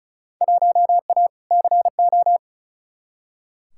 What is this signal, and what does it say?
Morse code, heard as a keyed tone, about 36 wpm, 700 Hz. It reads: 1A CO
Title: Field Day Exchanges - 35wpm